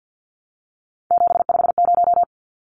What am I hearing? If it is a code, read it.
Morse code, heard as a keyed tone, about 50 wpm, 700 Hz. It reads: 750
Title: Numbers - 3-Digit 50wpm